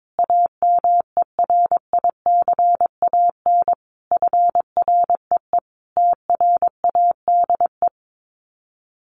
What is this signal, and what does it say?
Morse code, heard as a keyed tone, about 22 wpm, 700 Hz. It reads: AMERICAN FREE TRADE
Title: Sets of 5 Words Encore 22wpm